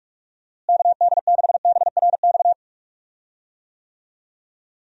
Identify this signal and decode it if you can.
Morse code, heard as a keyed tone, about 41 wpm, 700 Hz. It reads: KD6BRX
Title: USA Call Signs - 6 Characters - Repeated 3x - 45wpm